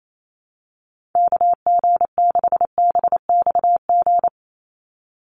Morse code, heard as a keyed tone, about 28 wpm, 700 Hz. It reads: KG6BXG
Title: USA Call Signs 28wpm